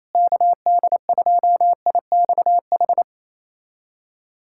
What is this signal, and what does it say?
Morse code, heard as a keyed tone, about 28 wpm, 700 Hz. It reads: KD2IXH
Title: USA Callsign DE Callsign - 6 Characters - 28wpm